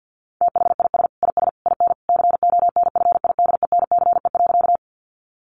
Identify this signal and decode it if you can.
Morse code, heard as a keyed tone, about 50 wpm, 700 Hz. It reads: THIS IS IN CONFIDENCE.
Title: Sentences from Top 1500 Words 50wpm